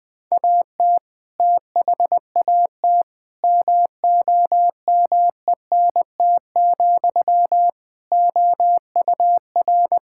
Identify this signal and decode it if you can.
Morse code, heard as a keyed tone, about 20 wpm, 700 Hz. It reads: AT THAT MOMENT, OUR
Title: Sentences from Top 1600 Words 20wpm